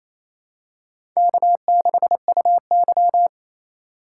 Morse code, heard as a keyed tone, about 28 wpm, 700 Hz. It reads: K6UY
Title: USA Call Signs - 4 Characters - 2x Character Spacing - 28wpm